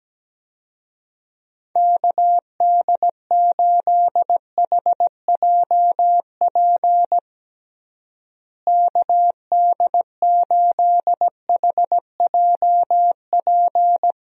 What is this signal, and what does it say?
Morse code, heard as a keyed tone, about 17 wpm, 700 Hz. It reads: KD8HJP KD8HJP
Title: USA Call Signs - 6 Characters - Repeated 2x - 17wpm